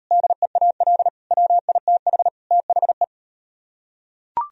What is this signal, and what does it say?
Morse code, heard as a keyed tone, about 38 wpm, 700 Hz. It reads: DEAL WITH THE